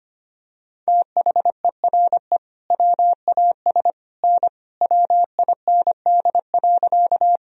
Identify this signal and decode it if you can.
Morse code, heard as a keyed tone, about 25 wpm, 700 Hz. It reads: THERE WAS N WIND.
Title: Sentences from Top 400 Words and Abbreviations 25wpm